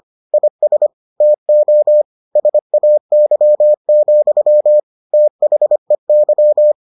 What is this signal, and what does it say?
Morse code, heard as a keyed tone, about 25 wpm, 600 Hz. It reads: IS TO SAY, THEY